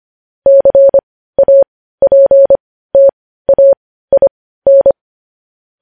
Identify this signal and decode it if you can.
Morse code, heard as a keyed tone, about 25 wpm, spaced as for 15 wpm, 550 Hz. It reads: CAPTAIN
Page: news-headlines-in-morse-code-at-15-wpm